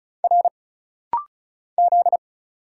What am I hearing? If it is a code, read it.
Morse code, heard as a keyed tone, about 35 wpm, 700 Hz. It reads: R Z
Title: Single Letters - Rapid-Fire 35wpm